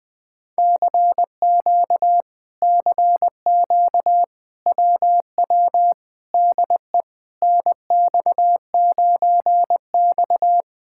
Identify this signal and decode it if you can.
Morse code, heard as a keyed tone, about 20 wpm, 700 Hz. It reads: CQ CQ WW DE NX9X
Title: Speed-Racing Common QSO Phrases - 17wpm